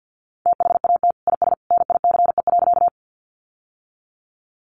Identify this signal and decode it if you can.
Morse code, heard as a keyed tone, about 50 wpm, 700 Hz. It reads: THAT IS NICE.
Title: Sentences from Top 1100 Words 50wpm